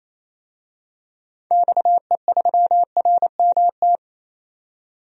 Morse code, heard as a keyed tone, about 28 wpm, 700 Hz. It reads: XE3RMT
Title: DXCC-100 Call Signs - 6 Characters - 28wpm